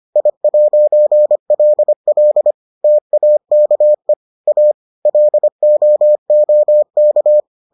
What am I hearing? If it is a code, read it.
Morse code, heard as a keyed tone, about 25 wpm, 600 Hz. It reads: I'LL TAKE A LOOK